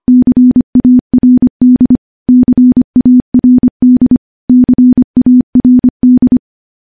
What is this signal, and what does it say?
Morse code, heard as a keyed tone, about 25 wpm, 250 Hz. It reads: CARD CARD CARD
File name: morse-training-25WPM